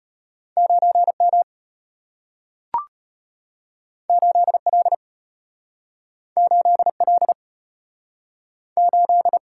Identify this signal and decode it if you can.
Morse code, heard as a keyed tone, about 34 wpm, 700 Hz. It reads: 9M 8L 8L 8